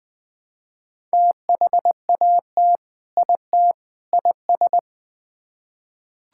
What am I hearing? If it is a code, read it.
Morse code, heard as a keyed tone, about 20 wpm, 700 Hz. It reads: THAT IT IS
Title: Build A Phrase - 3-Word Phrases - 20wpm